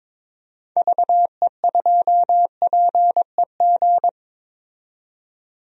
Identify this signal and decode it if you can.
Morse code, heard as a keyed tone, about 22 wpm, 700 Hz. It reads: VE2PEG
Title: DXCC-100 Call Signs - 6 Characters - 22wpm